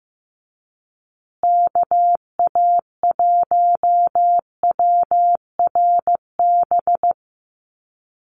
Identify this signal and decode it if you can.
Morse code, heard as a keyed tone, about 15 wpm, 700 Hz. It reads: KA1WRB
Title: USA Call Signs Repeated 3x 15wpm